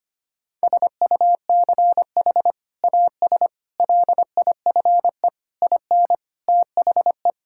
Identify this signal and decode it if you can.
Morse code, heard as a keyed tone, about 25 wpm, 700 Hz. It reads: SUCH AS LIFE IN THE
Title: Sentences from Top 1000 Words - Review Part 5 - 25wpm